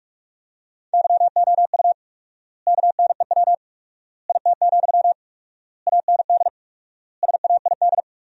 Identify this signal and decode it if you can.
Morse code, heard as a keyed tone, about 42 wpm, 700 Hz. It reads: YOU KNEW IT, AND SAID
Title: 4x Word-Spacing - Sentences from Top 500 Words - 45wpm